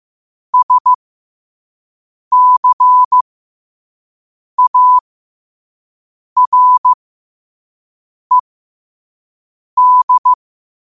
Morse code, heard as a keyed tone, about 15 wpm, spaced as for 5 wpm, 1000 Hz. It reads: SCARED
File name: qotdfiles / 5wpm